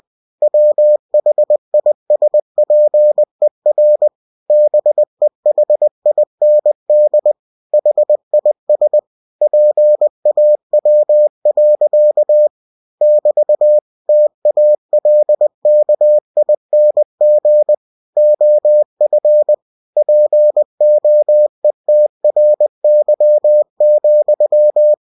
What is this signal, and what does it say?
Morse code, heard as a keyed tone, about 20 wpm, 600 Hz. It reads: WHISPER BEHIND HIS PAW. <BT> TALKING OF POETRY,